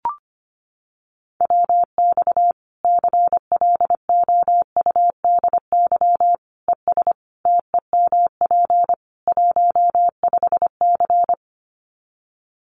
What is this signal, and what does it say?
Morse code, heard as a keyed tone, about 25 wpm, 700 Hz. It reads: WX CLOUDY ES TEMP 15C
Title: Common QSO Phrases 25wpm